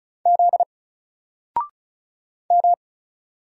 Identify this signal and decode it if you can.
Morse code, heard as a keyed tone, about 36 wpm, 700 Hz. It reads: Z M
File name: Single-Letter-Number-35wpm